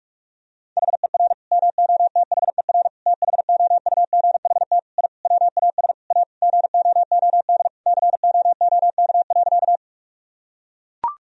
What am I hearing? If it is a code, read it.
Morse code, heard as a keyed tone, about 41 wpm, 700 Hz. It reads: HER MOTHER THOUGHT I WAS A GOOD COOK.
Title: Sentences from Top 1000 Words - Review Part 4 - 45wpm